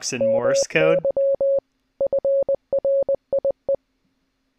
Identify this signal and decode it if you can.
Morse code, heard as a keyed tone, about 20 wpm, 550 Hz, with speech. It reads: MY FRIE